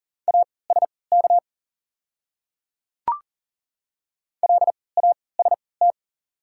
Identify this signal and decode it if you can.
Morse code, heard as a keyed tone, about 40 wpm, spaced as for 23 wpm, 690 Hz. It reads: ASK LAST
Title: 3x Character Spacing - Top 500 Words - 40wpm